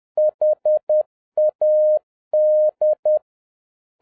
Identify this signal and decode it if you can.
Morse code, heard as a keyed tone, about 10 wpm, 610 Hz. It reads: HAD